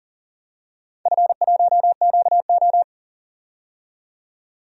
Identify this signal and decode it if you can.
Morse code, heard as a keyed tone, about 40 wpm, 700 Hz. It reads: F1QO
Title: DXCC-100 Call Signs - 4 Characters - 40wpm